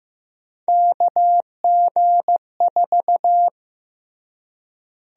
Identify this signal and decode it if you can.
Morse code, heard as a keyed tone, about 15 wpm, 700 Hz. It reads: KG4
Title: USA Call Sign Prefixes - Repeated 2x - 15wpm